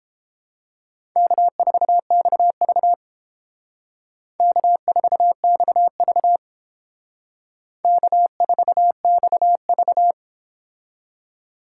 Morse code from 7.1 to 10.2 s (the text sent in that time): K4XV